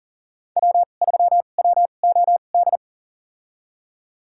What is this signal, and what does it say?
Morse code, heard as a keyed tone, about 40 wpm, spaced as for 29 wpm, 700 Hz. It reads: W3WOD
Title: USA Call Signs - 5 Characters - 2x Character Spacing - 40wpm